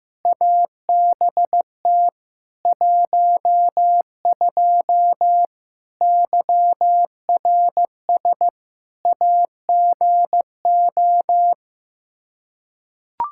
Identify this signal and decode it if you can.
Morse code, heard as a keyed tone, about 15 wpm, 700 Hz. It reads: ABT 12 YRS AGO
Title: K1USN SST QSO Practice 15wpm